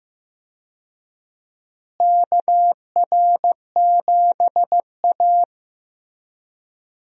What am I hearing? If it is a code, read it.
Morse code, heard as a keyed tone, about 16 wpm, 700 Hz. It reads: KR7A
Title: USA Call Signs - 4 Characters - Repeated 2x - 15wpm